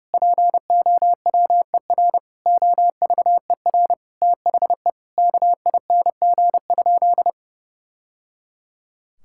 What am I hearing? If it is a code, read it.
Morse code, heard as a keyed tone, about 32 wpm, 700 Hz. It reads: POWER OVER THE KING?
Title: Sentences from Top 1000 Words - Review Part 6 - 30wpm